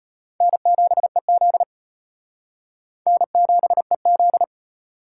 Morse code, heard as a keyed tone, about 31 wpm, 700 Hz. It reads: N7EZ N7EZ
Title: Speed Racing - USA 1x2 Call Signs 25wpm